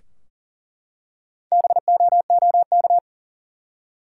Morse code, heard as a keyed tone, about 40 wpm, 700 Hz. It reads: BOOK
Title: Four Letter Words - 40wpm